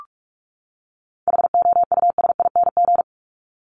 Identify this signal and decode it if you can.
Morse code, heard as a keyed tone, about 44 wpm, 700 Hz. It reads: HOUSING
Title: ING Suffix - Top 100 Words with Base Words 45wpm